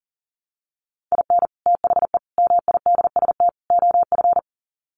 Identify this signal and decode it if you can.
Morse code, heard as a keyed tone, about 40 wpm, 700 Hz. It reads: IN THE MIDST OF
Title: Sets of 4 Words 40wpm